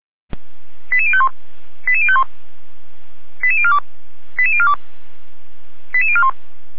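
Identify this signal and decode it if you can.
Five-tone selective call (ZVEI), heard as quick runs of five tones, tones of about 80 ms. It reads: 88051 88051 88042 88042 88041
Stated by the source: ZVEI five-tone call received by SDR, from server of feuerwehr-orlamuende.org (fire brigade Orlamünde).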